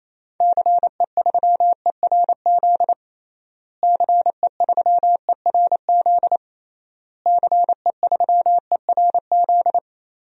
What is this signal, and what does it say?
Morse code, heard as a keyed tone, about 28 wpm, 700 Hz. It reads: CE3ERZ CE3ERZ CE3ERZ
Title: DXCC-100 Call Signs - 6 Characters - Repeated 3x - 28wpm